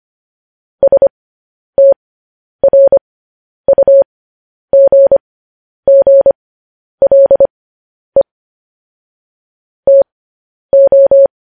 Morse code, heard as a keyed tone, about 25 wpm, spaced as for 10 wpm, 550 Hz. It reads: STRUGGLE TO